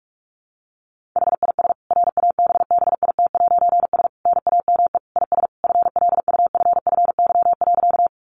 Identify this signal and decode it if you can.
Morse code, heard as a keyed tone, about 44 wpm, 700 Hz. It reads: HIS RABBIT'S NAME IS FLUFFY.
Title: Easy Phrases 45wpm